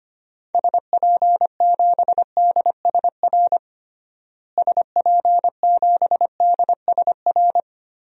Morse code, heard as a keyed tone, about 25 wpm, 700 Hz. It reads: SP7DSR SP7DSR
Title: DXCC-100 Call Signs - 6 Characters - Repeated 3x - 25wpm